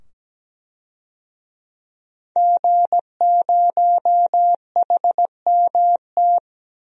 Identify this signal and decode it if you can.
Morse code, heard as a keyed tone, about 17 wpm, 700 Hz. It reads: G0HMT
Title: DXCC-100 Call Signs - 5 Characters - 17wpm